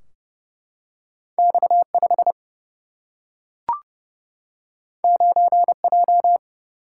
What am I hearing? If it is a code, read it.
Morse code, heard as a keyed tone, about 31 wpm, 700 Hz. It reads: X 5 9 J
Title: Single-Letter Single-Number 30wpm